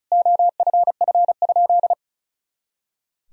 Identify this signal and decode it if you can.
Morse code, heard as a keyed tone, about 36 wpm, 700 Hz. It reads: OFF?
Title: Sentences from Top 500 Words and Abbreviations 35wpm